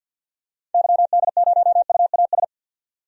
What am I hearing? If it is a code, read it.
Morse code, heard as a keyed tone, about 52 wpm, 700 Hz. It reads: YD0UAS
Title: DXCC-100 Call Signs - 6 Characters - 50wpm